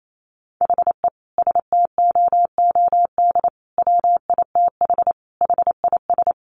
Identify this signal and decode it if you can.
Morse code, heard as a keyed tone, about 28 wpm, 700 Hz. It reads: HE STOOD WITH HIS
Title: Sentences from Top 1500 Words 28wpm